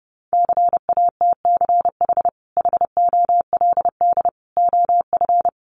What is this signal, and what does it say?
Morse code, heard as a keyed tone, about 30 wpm, 700 Hz. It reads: CATCH HOLD OF
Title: Sentences from Top 1500 Words 30wpm